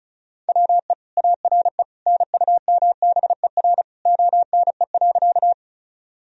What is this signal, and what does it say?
Morse code, heard as a keyed tone, about 36 wpm, 700 Hz. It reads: WE ARE NUMBER ONE.
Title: Sentences from Top 1000 Words - Review Part 1 - 35wpm